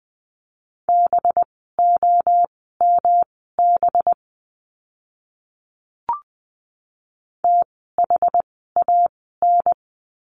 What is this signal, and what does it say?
Morse code, heard as a keyed tone, about 20 wpm, spaced as for 15 wpm, 700 Hz. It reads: BOMB THAN
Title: 185 Lesson 30f - Letter b - Words 20wpm